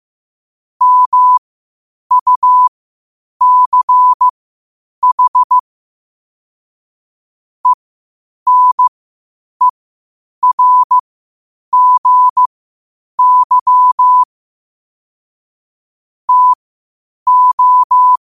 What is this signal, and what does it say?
Morse code, heard as a keyed tone, about 15 wpm, spaced as for 8 wpm, 1000 Hz. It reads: MUCH ENERGY TO
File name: qotdfiles / 8wpm